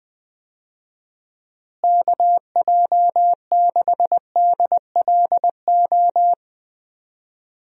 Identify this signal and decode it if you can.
Morse code, heard as a keyed tone, about 20 wpm, 700 Hz. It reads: KJ6DLO
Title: USA Call Signs - 6 Characters - Repeated 3x - 20wpm